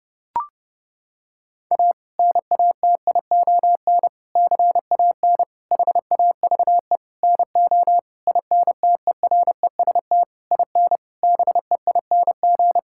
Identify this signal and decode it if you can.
Morse code, heard as a keyed tone, about 30 wpm, 700 Hz. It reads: A NATION CAN HAVE NO INTEREST IN BEING
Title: Sentences from Top 1000 Words - Review Part 6 - 30wpm